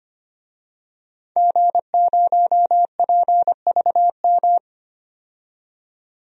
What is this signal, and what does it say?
Morse code, heard as a keyed tone, about 25 wpm, 700 Hz. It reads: G0PVM
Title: DXCC-100 Call Signs - 5 Characters - 25wpm